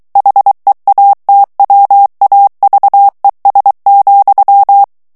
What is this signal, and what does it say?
Morse code, heard as a keyed tone, about 23 wpm, 800 Hz. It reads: HEATWAVES,
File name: morse mp3 / ANGLAIS texte clair 2 vitesse 22 mots minute